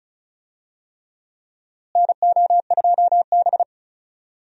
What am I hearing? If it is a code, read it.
Morse code, heard as a keyed tone, about 34 wpm, 700 Hz. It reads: NO2B
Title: USA Call Signs - 4 Characters - Repeated 3x - 35wpm